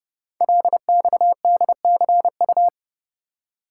Morse code, heard as a keyed tone, about 28 wpm, 700 Hz. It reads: LXDCU
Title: Code Groups - 5 Letters - 30wpm